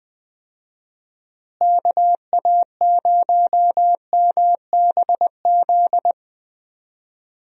Morse code, heard as a keyed tone, about 20 wpm, 700 Hz. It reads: KA0MBZ
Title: USA Call Signs - 6 Characters - Repeated 2x - 20wpm